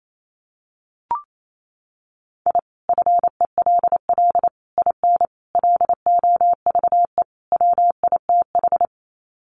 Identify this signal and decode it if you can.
Morse code, heard as a keyed tone, about 28 wpm, 700 Hz. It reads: I FELL IN LOVE WITH
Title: Sets of 5 Words Encore 28wpm